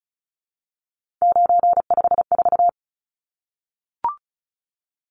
Morse code, heard as a keyed tone, about 34 wpm, 700 Hz. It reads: 954
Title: Numbers - 3-Digit 35wpm